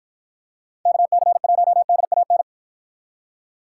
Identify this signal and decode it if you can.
Morse code, heard as a keyed tone, about 53 wpm, 700 Hz. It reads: KC1DAN